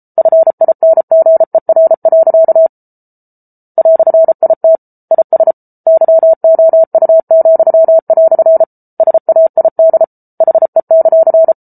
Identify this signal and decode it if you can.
Morse code, heard as a keyed tone, about 34 wpm, 650 Hz. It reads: FINGER. "IT IS YOU," SAID HE;